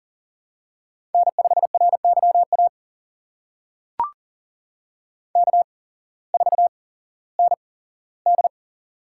Morse code, heard as a keyed tone, about 42 wpm, spaced as for 19 wpm, 700 Hz. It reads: N5RYA K4ND
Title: USA Call Signs - 8x Character Spacing - 40wpm